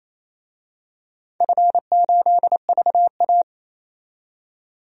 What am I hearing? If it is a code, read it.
Morse code, heard as a keyed tone, about 27 wpm, 700 Hz. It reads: F8VA
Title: DXCC-100 Call Signs - 4 Characters - 28wpm